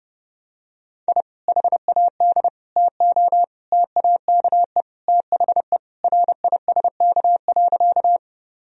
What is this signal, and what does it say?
Morse code, heard as a keyed tone, about 30 wpm, 700 Hz. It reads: I HAD TO TAKE THE RISK.
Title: Sentences from Top 1000 Words - Review Part 4 - 30wpm